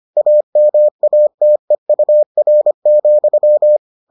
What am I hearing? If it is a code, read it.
Morse code, heard as a keyed tone, about 25 wpm, 600 Hz. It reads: AMATEUR,